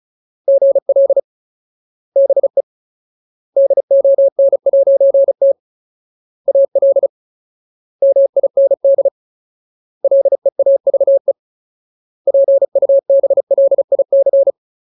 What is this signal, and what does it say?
Morse code, heard as a keyed tone, about 34 wpm, 550 Hz. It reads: GL BE DON'T AL MIND LEAVE PUBLIC
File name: Ultimate Word List / 35WPM Ultimate